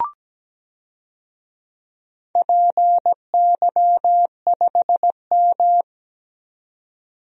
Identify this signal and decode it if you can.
Morse code, heard as a keyed tone, about 17 wpm, 700 Hz. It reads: PY5M
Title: DXCC-100 Call Signs - 4 Characters - 17wpm